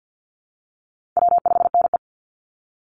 Morse code, heard as a keyed tone, about 48 wpm, 700 Hz. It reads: W5NE
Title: Call Signs Repeated 2x - 50wpm